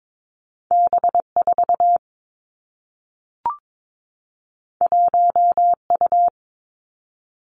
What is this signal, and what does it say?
Morse code, heard as a keyed tone, about 22 wpm, 700 Hz. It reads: B 4 1 U
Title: Two Challenging Letter-Number Combinations 22wpm